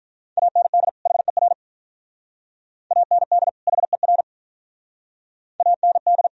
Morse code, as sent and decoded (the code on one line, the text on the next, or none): .- -. -.. / .... .-. / .- -. -.. / .... .-. / .- -. -..
AND HER AND HER AND